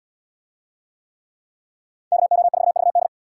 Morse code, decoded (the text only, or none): KO4VD